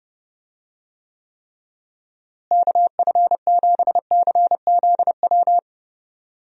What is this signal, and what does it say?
Morse code, heard as a keyed tone, about 32 wpm, 700 Hz. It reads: KF7CZW
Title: USA Call Signs - 6 Characters - Repeated 3x - 30wpm